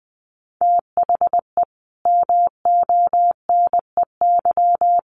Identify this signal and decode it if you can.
Morse code, heard as a keyed tone, about 19 wpm, 700 Hz. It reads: THE MONEY
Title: Sentences from Top 1400 Words 20wpm